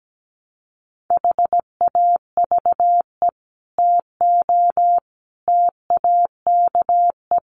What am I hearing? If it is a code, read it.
Morse code, heard as a keyed tone, about 17 wpm, 690 Hz. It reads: HAVE TO TAKE
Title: Sets of 5 Words Encore 17wpm